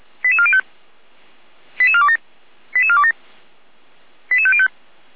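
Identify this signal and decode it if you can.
Five-tone selective call (ZVEI), heard as quick runs of five tones, tones of about 70 ms. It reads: 88405 88427 88427 88505
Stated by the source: ZVEI five-tone call received by SDR, from Feuerwehr Orlamünde audio server.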